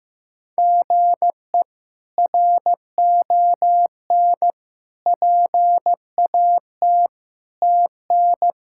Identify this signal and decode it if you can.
Morse code, heard as a keyed tone, about 16 wpm, 700 Hz. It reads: GE RON PAT TN
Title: K1USN SST QSO Practice - 15wpm